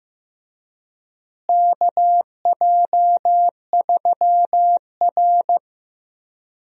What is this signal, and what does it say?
Morse code, heard as a keyed tone, about 15 wpm, 700 Hz. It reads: KJ3R
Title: USA Call Signs - 4 Characters - Repeated 3x - 15wpm